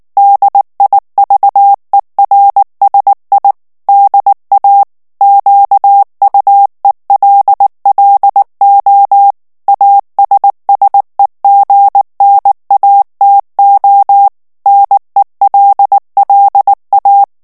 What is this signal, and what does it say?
Morse code, heard as a keyed tone, about 19 wpm, 800 Hz. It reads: DIVERSI DA QUELLO ASSEGNATO NELLA
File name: morse mp3 / ITALIEN texte clair 2 vitesse 18 mots minute